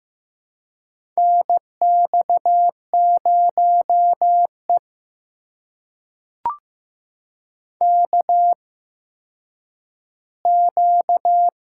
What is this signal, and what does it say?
Morse code, heard as a keyed tone, about 15 wpm, spaced as for 9 wpm, 700 Hz. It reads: NX0E KQ
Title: USA Call Signs - 4 Characters - 8x Character Spacing - 15wpm